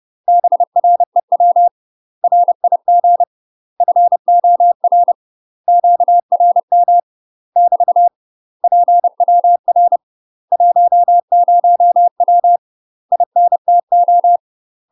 Morse code, heard as a keyed tone, about 30 wpm, 700 Hz. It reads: BREW RIG FOR QRM <BT> PWR 10W INTO
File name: UK MASTER QSOs TRAINING / 30WPM / UK MASTER QSOs 0040